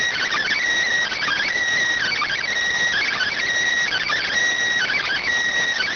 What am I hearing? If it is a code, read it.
Emergency Alert System (SAME) tones, heard as harsh buzzing bursts. It is heard from 1.1 to 5.2 s.